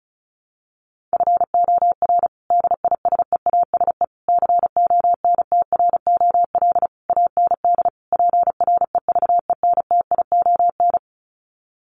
Morse code, heard as a keyed tone, about 34 wpm, 700 Hz. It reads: FOR DISEASE CONTROL AND PREVENTION
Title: Sets of 5 Words Encore2 35wpm